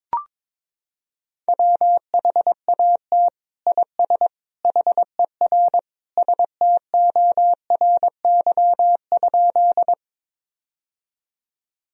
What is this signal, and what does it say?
Morse code, heard as a keyed tone, about 22 wpm, 700 Hz. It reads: WHAT IS HER STORY?
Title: Sentences from Top 1000 Words - Review Part 1 - 22wpm